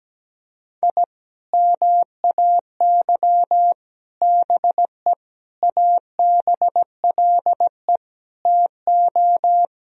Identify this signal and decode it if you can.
Morse code, heard as a keyed tone, about 17 wpm, 700 Hz. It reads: I MAY BE ABLE TO
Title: Sentences from Top 1000 Words - Review Part 2 - 17wpm